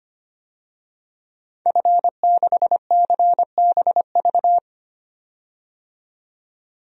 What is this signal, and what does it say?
Morse code, heard as a keyed tone, about 25 wpm, 700 Hz. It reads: F6CBV
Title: DXCC-100 Call Signs - 5 Characters - 25wpm